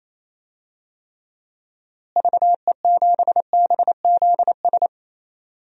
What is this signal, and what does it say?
Morse code, heard as a keyed tone, about 27 wpm, 700 Hz. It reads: VE7BZS
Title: DXCC-100 Call Signs - 6 Characters - Repeated 2x - 28wpm